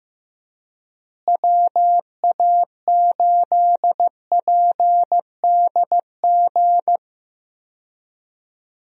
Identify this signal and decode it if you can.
Morse code, heard as a keyed tone, about 15 wpm, 700 Hz. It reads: WA8PDG
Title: USA Call Signs - 6 Characters - Repeated 3x - 15wpm